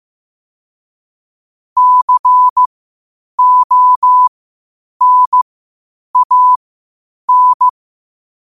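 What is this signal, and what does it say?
Morse code, heard as a keyed tone, about 15 wpm, spaced as for 8 wpm, 1000 Hz. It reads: CONAN
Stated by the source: Morse code quotes at 8 words per minute from QOTD at 8 WPM.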